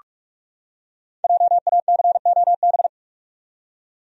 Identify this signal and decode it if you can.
Morse code, heard as a keyed tone, about 44 wpm, 700 Hz. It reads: JACOB
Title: USA Common Names - 45wpm